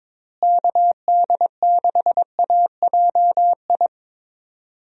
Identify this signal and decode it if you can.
Morse code, heard as a keyed tone, about 22 wpm, 700 Hz. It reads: KD6AJI
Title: USA Call Signs - 6 Characters - Repeated 2x - 22wpm